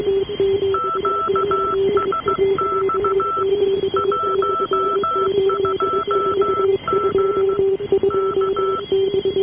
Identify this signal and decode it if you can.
Morse code, heard as a keyed tone, about 15 wpm, 1330 Hz, with noise. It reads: QSX 12M K
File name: unm2 12703 1039